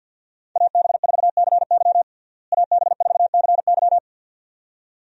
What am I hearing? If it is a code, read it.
Morse code, heard as a keyed tone, about 52 wpm, 700 Hz. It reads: AB4CY AB4CY
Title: USA Call Signs - 5 Characters - Repeated 2x - 50wpm